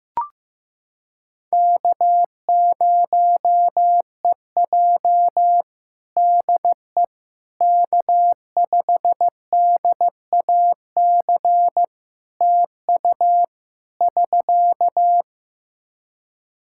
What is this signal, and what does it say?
Morse code, heard as a keyed tone, about 15 wpm, 700 Hz. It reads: K0EJ DE K5DAC TU <SK>
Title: Practice QSO - Exchange 4 - Ending - 15wpm